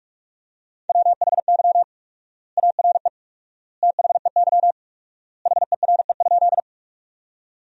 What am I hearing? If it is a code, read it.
Morse code, heard as a keyed tone, about 44 wpm, 700 Hz. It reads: WHY ARE THEY HERE?
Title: 4x Word-Spacing - Sentences from Top 200 Words - 45wpm